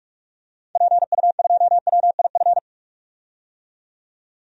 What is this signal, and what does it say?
Morse code, heard as a keyed tone, about 47 wpm, 700 Hz. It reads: PU2WIF
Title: DXCC-100 Call Signs - 6 Characters - 45wpm